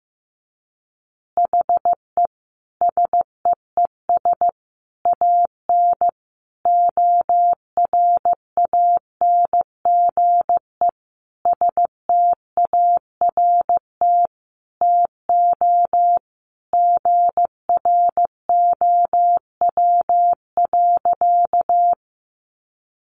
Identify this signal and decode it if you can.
Morse code, heard as a keyed tone, about 15 wpm, 700 Hz. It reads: HE SEES AN ORANGE START TO GROW.
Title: Easy Phrases 15wpm